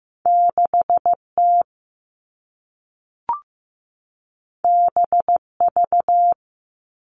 Morse code, heard as a keyed tone, about 15 wpm, 700 Hz. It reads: 6T BV
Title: All Cut Numbers - 2-Digit 15wpm